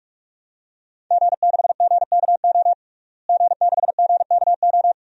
Morse code, heard as a keyed tone, about 42 wpm, 700 Hz. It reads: G6GKO G6GKO
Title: DXCC-100 Call Signs - 5 Characters - Repeated 3x - 45wpm